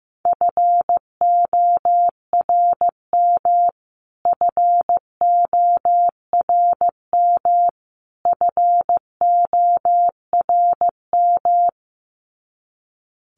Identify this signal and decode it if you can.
Morse code, heard as a keyed tone, about 15 wpm, 700 Hz. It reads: FORM FORM FORM
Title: Top 500 Words Repeated 15wpm